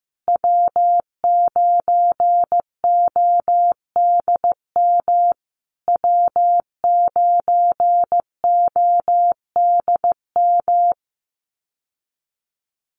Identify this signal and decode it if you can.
Morse code, heard as a keyed tone, about 15 wpm, 700 Hz. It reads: W9ODM W9ODM
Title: USA Call Signs Repeated 3x 15wpm